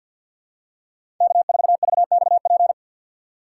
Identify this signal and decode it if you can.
Morse code, heard as a keyed tone, about 53 wpm, 700 Hz. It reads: K4VXP
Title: USA Call Signs - 2x Character Spacing - 50wpm